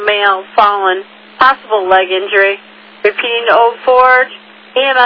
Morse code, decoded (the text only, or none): none